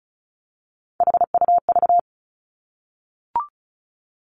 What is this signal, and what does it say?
Morse code, heard as a keyed tone, about 36 wpm, 700 Hz. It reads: HUV
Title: Three Challenging Letter-Number Combinations 35wpm